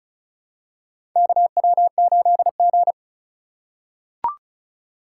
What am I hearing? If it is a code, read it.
Morse code, heard as a keyed tone, about 36 wpm, 700 Hz. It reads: KW8G
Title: USA Call Signs - 4 Characters - 2x Character Spacing - 35wpm